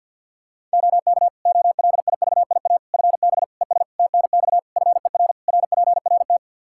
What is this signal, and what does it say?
Morse code, heard as a keyed tone, about 49 wpm, 700 Hz. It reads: OK OLIVIA FB ES TNX FER RPRT
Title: Practice QSO - Exchange 2 - Info - 50wpm